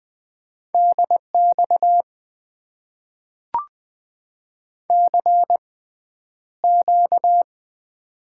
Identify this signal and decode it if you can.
Morse code, heard as a keyed tone, about 20 wpm, spaced as for 10 wpm, 700 Hz. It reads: DX CQ
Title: Top 90 QSO Elements - 6x Character Spacing - 20wpm